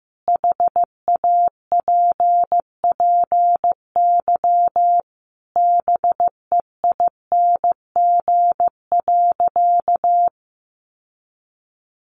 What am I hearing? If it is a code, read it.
Morse code, heard as a keyed tone, about 15 wpm, 700 Hz. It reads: HAPPY BEING.
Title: Sentences from Top 1000 Words 15wpm